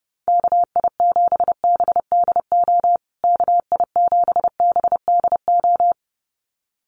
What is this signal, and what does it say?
Morse code, heard as a keyed tone, about 30 wpm, 700 Hz. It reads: KI7BDO KI7BDO
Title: USA Call Signs Repeated 2x 30wpm